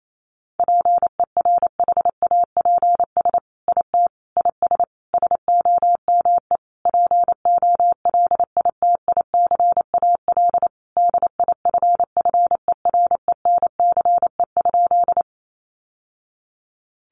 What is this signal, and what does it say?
Morse code, heard as a keyed tone, about 28 wpm, 700 Hz. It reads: PERHAPS IT IS SOME POLITICAL DIFFERENCE?
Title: Sentences from Top 800 Words 28wpm